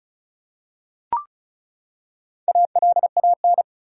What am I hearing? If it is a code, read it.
Morse code, heard as a keyed tone, about 34 wpm, 700 Hz. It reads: ALAN
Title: USA Common Names - Max 4 Letters - 35wpm